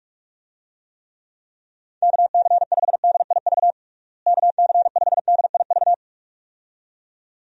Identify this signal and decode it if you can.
Morse code, heard as a keyed tone, about 42 wpm, 700 Hz. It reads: KC5DIV KC5DIV
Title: USA Call Signs - 6 Characters - Repeated 2x - 45wpm